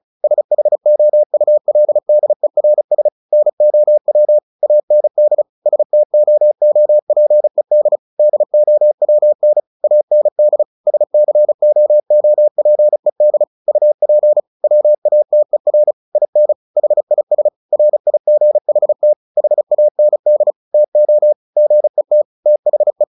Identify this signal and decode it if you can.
Morse code, heard as a keyed tone, about 34 wpm, 600 Hz. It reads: SHOULDERS NOW AND STOOPED DOWN AND SCOOPED UP WATER IN HIS RIGHT HAND TO GET THE
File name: the old man and the sea 2 35wpm